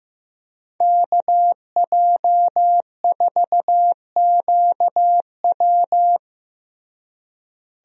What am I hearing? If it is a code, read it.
Morse code, heard as a keyed tone, about 15 wpm, 700 Hz. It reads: KJ4QW
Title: USA Call Signs - 5 Characters - Repeated 3x - 15wpm